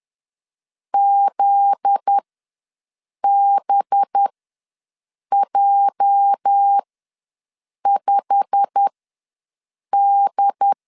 Morse code, heard as a keyed tone, about 10 wpm, spaced as for 6 wpm, 800 Hz. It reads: ZBJ5D
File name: LEZIONI / 9 28 ws 10wpm 4124